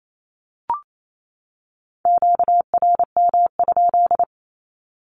Morse code, heard as a keyed tone, about 28 wpm, 700 Hz. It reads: QRM?
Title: Q-Codes 28wpm